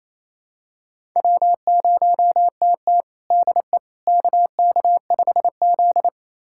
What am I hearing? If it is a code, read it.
Morse code, heard as a keyed tone, about 28 wpm, 700 Hz. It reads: W0TT DE KK5Z
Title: USA Callsign DE Callsign - 4 Characters - 28wpm